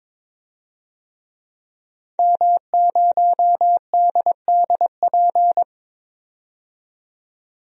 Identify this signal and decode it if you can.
Morse code, heard as a keyed tone, about 22 wpm, 700 Hz. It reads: M0DDP
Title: DXCC-100 Call Signs - 5 Characters - 22wpm